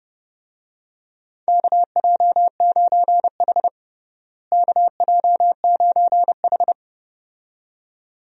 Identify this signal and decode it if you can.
Morse code, heard as a keyed tone, about 32 wpm, 700 Hz. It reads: KJ9H KJ9H
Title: USA Call Signs - 4 Characters - Repeated 2x - 30wpm